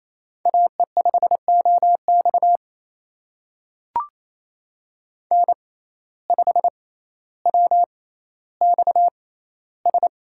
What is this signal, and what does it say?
Morse code, heard as a keyed tone, about 28 wpm, spaced as for 14 wpm, 700 Hz. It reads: AE5OX N5WXS
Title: USA Call Signs - 5 Characters - 6x Character Spacing - 28wpm